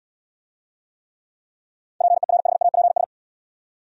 Morse code, heard as a keyed tone, about 67 wpm, 690 Hz. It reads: PERHAPS